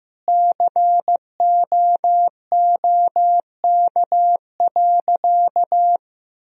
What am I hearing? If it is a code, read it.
Morse code, heard as a keyed tone, about 15 wpm, 700 Hz. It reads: COOK.
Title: Sentences from Top 1000 Words - Review Part 4 - 15wpm